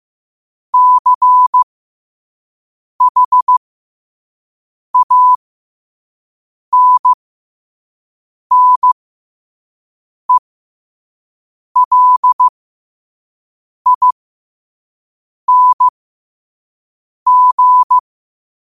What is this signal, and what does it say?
Morse code, heard as a keyed tone, about 15 wpm, spaced as for 5 wpm, 1000 Hz. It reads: CHANNELING
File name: 5wpm